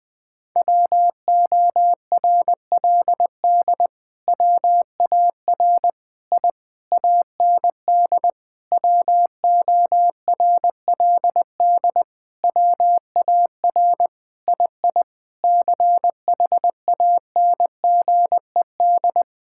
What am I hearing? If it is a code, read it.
Morse code, heard as a keyed tone, about 19 wpm, 700 Hz. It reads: WORLD WAR I AND WORLD WAR II CHANGED
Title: Sentences with 7-Word Phrases - 20wpm